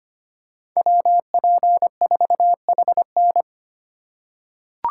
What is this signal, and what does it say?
Morse code, heard as a keyed tone, about 25 wpm, 700 Hz. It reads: WP4HN
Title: USA Call Signs - 5 Characters - 2x Character Spacing - 25wpm